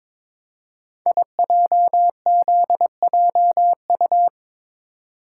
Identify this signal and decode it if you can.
Morse code, heard as a keyed tone, about 22 wpm, 700 Hz. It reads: IJZJU
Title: Code Groups - 5 Letter:Numbers - 22wpm